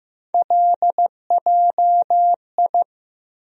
Morse code, heard as a keyed tone, about 15 wpm, 700 Hz. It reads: LJI
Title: USA Call Sign Postfixes - 3 Letters 15wpm